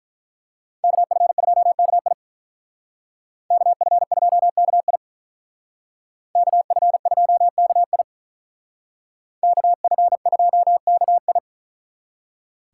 Morse code, heard as a keyed tone, about 42 wpm, 700 Hz. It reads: KF2KI KF2KI KF2KI KF2KI
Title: Speed Racing - USA 2x2 Call Signs 35wpm